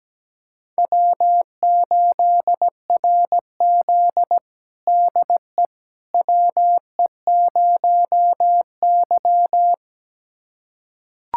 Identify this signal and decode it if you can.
Morse code, heard as a keyed tone, about 17 wpm, 700 Hz. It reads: W8RZ DE WE0Y
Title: USA Callsign DE Callsign - 4 Characters - 17wpm